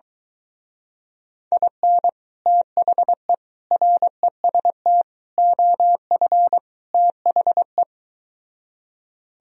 Morse code, heard as a keyed tone, about 23 wpm, 700 Hz. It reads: IN THE REST OF THE